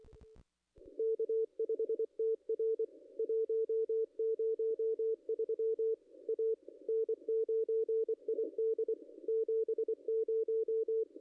Morse code, heard as a keyed tone, about 24 wpm, 440 Hz, with radio noise. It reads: K5TR 103 A N9ID 70
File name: contests / ss-cw-k5tr-op-n5rz -0200